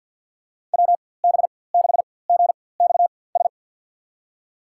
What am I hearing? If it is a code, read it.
Morse code, heard as a keyed tone, about 52 wpm, spaced as for 23 wpm, 700 Hz. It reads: WB6GXS
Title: USA Call Signs - 6 Characters - 4x Character Spacing - 50wpm